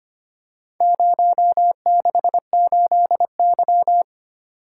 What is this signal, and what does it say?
Morse code, heard as a keyed tone, about 25 wpm, 700 Hz. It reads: 068Y
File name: practice / 25wpm / Four Letter-Number Combinations - 25wpm